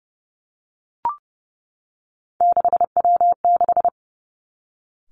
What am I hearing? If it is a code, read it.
Morse code, heard as a keyed tone, about 30 wpm, 700 Hz. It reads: 6W6
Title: Three Challenging Letter-Number Combinations 30wpm